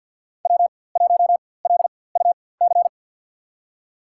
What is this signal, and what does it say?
Morse code, heard as a keyed tone, about 47 wpm, spaced as for 23 wpm, 700 Hz. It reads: W1LUC
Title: USA Call Signs - 4x Character Spacing - 50wpm